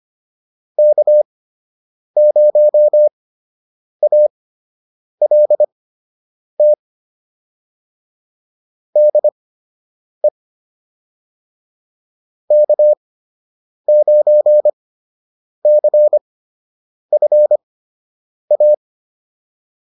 Morse code, heard as a keyed tone, about 25 wpm, spaced as for 8 wpm, 600 Hz. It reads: K0ALT DE K9CFA